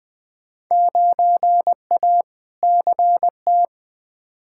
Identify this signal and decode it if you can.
Morse code, heard as a keyed tone, about 20 wpm, 700 Hz. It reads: 9A CT
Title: Field Day Exchanges Encore - 20wpm